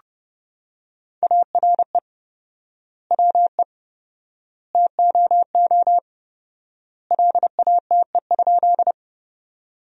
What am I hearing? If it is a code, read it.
Morse code, heard as a keyed tone, about 30 wpm, 700 Hz. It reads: ARE WE TOO LATE?
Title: 4x Word-Spacing - Sentences from Top 300 Words - 30wpm